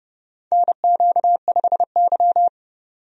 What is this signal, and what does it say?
Morse code, heard as a keyed tone, about 30 wpm, 700 Hz. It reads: NQ5Y
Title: USA Call Signs - 4 Characters - 30wpm